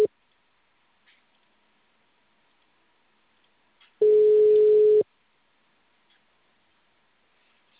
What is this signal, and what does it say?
Ringback tone (Europe, 425 Hz).